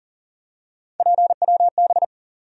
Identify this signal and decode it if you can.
Morse code, heard as a keyed tone, about 40 wpm, 700 Hz. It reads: PWB
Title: USA Call Sign Postfixes - 3 Letters 40wpm